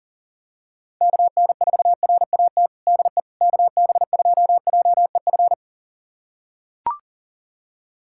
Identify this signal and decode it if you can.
Morse code, heard as a keyed tone, about 40 wpm, 700 Hz. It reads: KN4RAT DE KB2JEF
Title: USA Callsign DE Callsign - 6 Characters - 40wpm